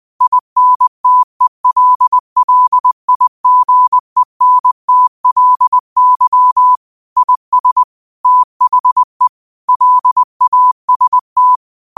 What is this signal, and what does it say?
Morse code, heard as a keyed tone, about 20 wpm, 1000 Hz. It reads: INTELLIGENTLY IS THE LAST